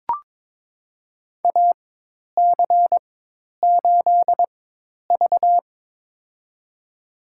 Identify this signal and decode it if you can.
Morse code, heard as a keyed tone, about 22 wpm, spaced as for 10 wpm, 700 Hz. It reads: AC8V
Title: USA Call Signs - 4 Characters - 4x Character Spacing - 22wpm